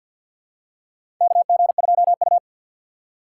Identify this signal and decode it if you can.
Morse code, heard as a keyed tone, about 53 wpm, 700 Hz. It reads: KG2U
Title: USA Call Signs - 4 Characters - 6x Character Spacing - 50wpm